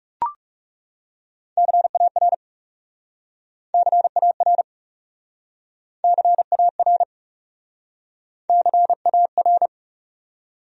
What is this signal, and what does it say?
Morse code, heard as a keyed tone, about 40 wpm, 700 Hz. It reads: CAR CAR CAR CAR